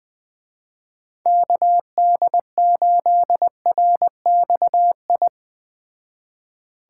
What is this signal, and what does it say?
Morse code, heard as a keyed tone, about 20 wpm, 700 Hz. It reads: KD8RXI
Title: Speed Racing - USA 2x3 Call Signs 17wpm